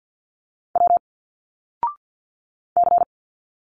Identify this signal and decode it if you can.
Morse code, heard as a keyed tone, about 51 wpm, 700 Hz. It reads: W C
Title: Single Letters 50wpm